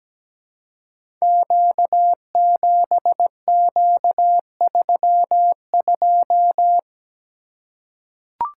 Morse code, heard as a keyed tone, about 17 wpm, 700 Hz. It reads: Q7Q32
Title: Code Groups - 5 Letter:Numbers - 17wpm